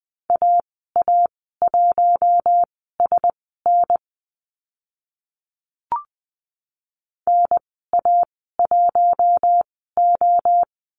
Morse code, heard as a keyed tone, about 20 wpm, spaced as for 15 wpm, 700 Hz. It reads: AA1SN NA1O
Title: 042 Lesson 8g - Number 1 - Call Signs 20wpm